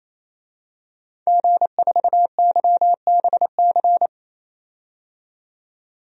Morse code, heard as a keyed tone, about 28 wpm, 700 Hz. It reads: G4YBC
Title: DXCC-100 Call Signs - 5 Characters - 28wpm